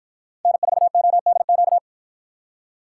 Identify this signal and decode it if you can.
Morse code, heard as a keyed tone, about 53 wpm, 700 Hz. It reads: N4ODQ